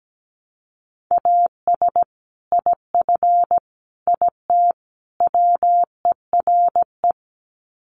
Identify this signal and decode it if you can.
Morse code, heard as a keyed tone, about 17 wpm, 700 Hz. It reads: AS IF IT WERE
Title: Sets of 4 Words 17wpm